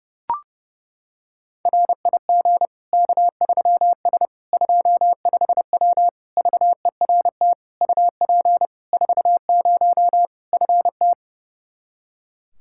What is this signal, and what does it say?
Morse code, heard as a keyed tone, about 30 wpm, 700 Hz. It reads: RIG K3S 25W VERT UP 40 FT
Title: K1USN SST QSO Practice 30wpm